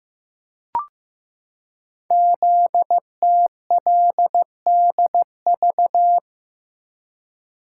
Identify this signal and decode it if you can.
Morse code, heard as a keyed tone, about 15 wpm, 700 Hz. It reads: ZTLDV
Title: Code Groups - 5 Letters - 15wpm